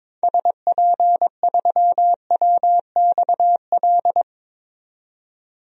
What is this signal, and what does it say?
Morse code, heard as a keyed tone, about 22 wpm, 700 Hz. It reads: SP3WXL
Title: DXCC-100 Call Signs - 6 Characters - 22wpm